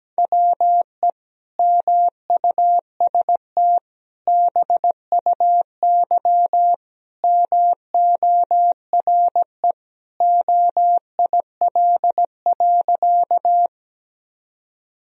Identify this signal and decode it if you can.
Morse code, heard as a keyed tone, about 17 wpm, 700 Hz. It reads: WE MUST BUY MORE OIL.
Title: Sentences from Top 1000 Words - Review Part 4 - 17wpm